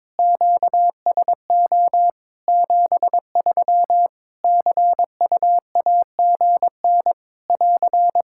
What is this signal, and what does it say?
Morse code, heard as a keyed tone, about 22 wpm, 700 Hz. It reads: QSO 73 CUAGN <AR>
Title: Practice QSO - Exchange 4 - Ending - 22wpm